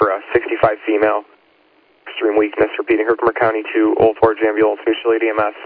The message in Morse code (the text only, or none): none